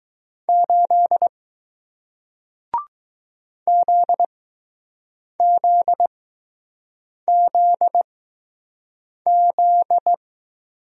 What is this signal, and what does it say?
Morse code, heard as a keyed tone, about 20 wpm, spaced as for 6 wpm, 700 Hz. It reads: 8 ZZZZ